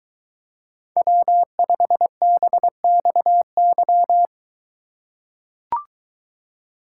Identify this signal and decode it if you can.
Morse code, heard as a keyed tone, about 21 wpm, 700 Hz. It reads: W5BXY